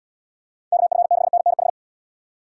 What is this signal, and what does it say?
Morse code, heard as a keyed tone, about 78 wpm, 700 Hz. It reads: KF6NNH